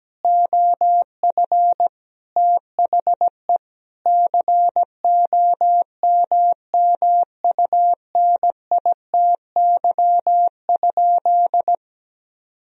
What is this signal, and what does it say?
Morse code, heard as a keyed tone, about 17 wpm, 700 Hz. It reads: OF THE COMMUNITY?
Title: Sentences from Top 1000 Words - Review Part 4 - 17wpm